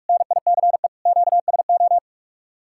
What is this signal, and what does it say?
Morse code, heard as a keyed tone, about 58 wpm, 700 Hz. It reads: NICE QSO